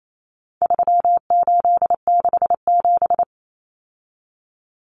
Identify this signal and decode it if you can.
Morse code, heard as a keyed tone, about 28 wpm, 700 Hz. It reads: 3867
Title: Numbers - 4-Digit 28wpm